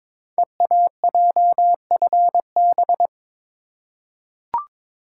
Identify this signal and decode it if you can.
Morse code, heard as a keyed tone, about 22 wpm, 700 Hz. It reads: EAJFB
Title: Code Groups - 5 Letter:Numbers - 22wpm